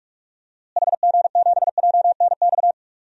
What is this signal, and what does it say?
Morse code, heard as a keyed tone, about 46 wpm, 700 Hz. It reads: HG7JNX
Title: DXCC-100 Call Signs - 6 Characters - Repeated 3x - 45wpm